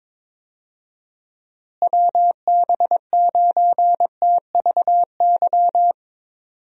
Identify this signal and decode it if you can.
Morse code, heard as a keyed tone, about 22 wpm, 700 Hz. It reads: WB9TVY
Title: USA Call Signs - 6 Characters - Repeated 2x - 22wpm